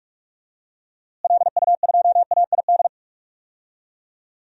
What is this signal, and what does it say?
Morse code, heard as a keyed tone, about 41 wpm, 700 Hz. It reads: LU2AID
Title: DXCC-100 Call Signs - 6 Characters - Repeated 3x - 45wpm